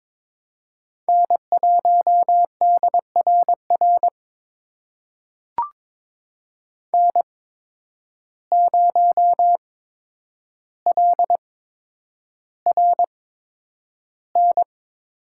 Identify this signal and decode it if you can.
Morse code, heard as a keyed tone, about 22 wpm, spaced as for 9 wpm, 700 Hz. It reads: N1DRR N0LRN
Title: USA Call Signs - 5 Characters - 8x Character Spacing - 22wpm